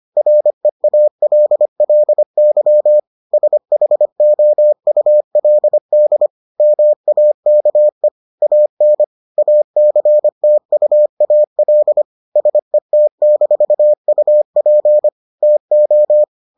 Morse code, heard as a keyed tone, about 25 wpm, 600 Hz. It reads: REALLY SHOULD MAKE AN ACTUAL SET-UP TO